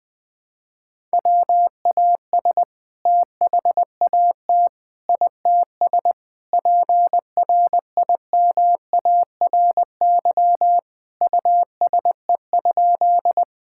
Morse code, heard as a keyed tone, about 21 wpm, 700 Hz. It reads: WAS THAT ITS PRIMARY USE?
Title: Sentences from Top 1000 Words - Review Part 5 - 20wpm